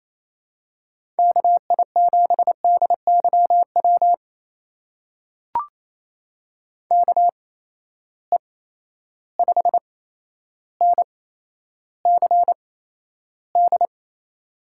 Morse code, heard as a keyed tone, about 28 wpm, spaced as for 12 wpm, 700 Hz. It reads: KI7DYW KE5NCD
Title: USA Call Signs - 6 Characters - 8x Character Spacing - 28wpm